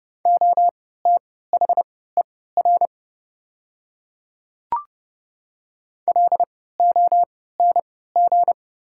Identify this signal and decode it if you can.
Morse code, heard as a keyed tone, about 32 wpm, spaced as for 17 wpm, 700 Hz. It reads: OTHER LONG
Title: 3x Character Spacing - Top 100 Words - 30wpm